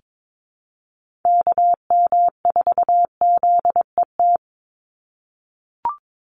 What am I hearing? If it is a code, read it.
Morse code, heard as a keyed tone, about 22 wpm, 700 Hz. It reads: KM4ZET
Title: USA Call Signs 22wpm